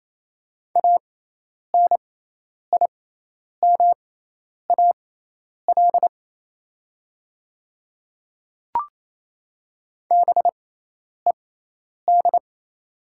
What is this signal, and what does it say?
Morse code, heard as a keyed tone, about 28 wpm, spaced as for 10 wpm, 700 Hz. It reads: ANIMAL BED
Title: 6x Character Spacing - Top 500 Words - 28wpm